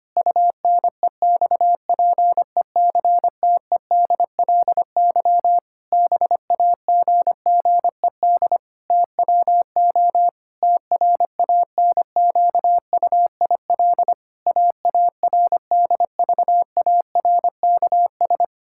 Morse code, heard as a keyed tone, about 25 wpm, 700 Hz. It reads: UNEXPECTEDLY BAGGED TWO TRANQUIL AARDVARKS
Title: Pangram Challenge v2 - 25wpm